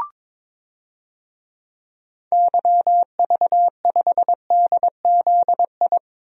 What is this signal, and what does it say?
Morse code, heard as a keyed tone, about 23 wpm, 700 Hz. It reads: YV5DZI
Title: DXCC-100 Call Signs - 6 Characters - 22wpm